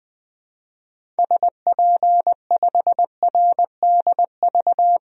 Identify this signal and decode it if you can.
Morse code, heard as a keyed tone, about 20 wpm, 700 Hz. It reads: SP5RDV
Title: DXCC-100 Call Signs - 6 Characters - Repeated 3x - 20wpm